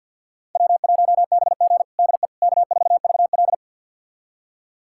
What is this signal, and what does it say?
Morse code, heard as a keyed tone, about 47 wpm, 700 Hz. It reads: W1BG DE K4VL
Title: USA Callsign DE Callsign - 4 Characters - 50wpm